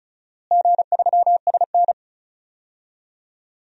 Morse code, heard as a keyed tone, about 36 wpm, 700 Hz. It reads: G3SN
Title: DXCC-100 Call Signs - 4 Characters - Repeated 2x - 35wpm